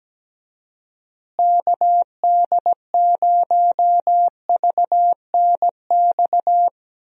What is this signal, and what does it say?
Morse code, heard as a keyed tone, about 17 wpm, 700 Hz. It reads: KD0VNX
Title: USA Call Signs - 6 Characters - Repeated 2x - 17wpm